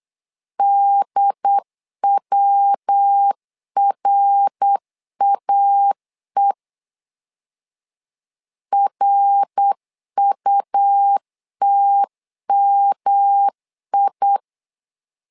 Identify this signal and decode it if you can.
Morse code, heard as a keyed tone, about 8 wpm, 800 Hz. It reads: DWRAE RUTMI
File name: LEZIONI / 3 MWRO / 3 MWR 8wpm 2444